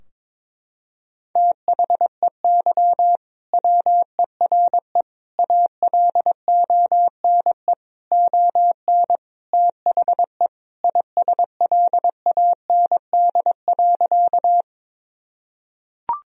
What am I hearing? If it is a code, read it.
Morse code, heard as a keyed tone, about 22 wpm, 700 Hz. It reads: THEY WERE ALONE ON THE ISLAND.
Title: Sentences from Top 1000 Words - Review Part 3 - 22wpm